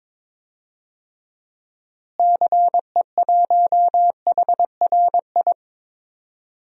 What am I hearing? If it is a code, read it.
Morse code, heard as a keyed tone, about 21 wpm, 700 Hz. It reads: CE1HRI
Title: DXCC-100 Call Signs - 6 Characters - Repeated 2x - 22wpm